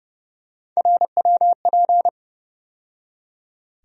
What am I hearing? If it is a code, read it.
Morse code, heard as a keyed tone, about 30 wpm, 700 Hz. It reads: RWP
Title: USA Call Sign Postfixes - 3 Letters 30wpm